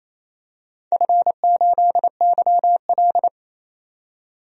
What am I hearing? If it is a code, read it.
Morse code, heard as a keyed tone, about 28 wpm, 700 Hz. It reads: F8YL
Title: DXCC-100 Call Signs - 4 Characters - 28wpm